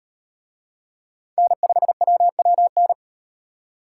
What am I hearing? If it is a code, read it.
Morse code, heard as a keyed tone, about 33 wpm, 700 Hz. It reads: N5WWN